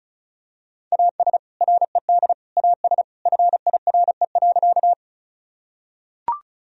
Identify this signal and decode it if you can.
Morse code, heard as a keyed tone, about 34 wpm, 700 Hz. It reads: AS RED AS FIRE.
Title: Sentences from Top 1000 Words - Review Part 2 - 35wpm